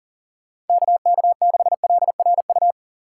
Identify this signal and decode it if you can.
Morse code, heard as a keyed tone, about 40 wpm, 700 Hz. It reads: KK6LRU
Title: USA Call Signs - 6 Characters - 2x Character Spacing - 40wpm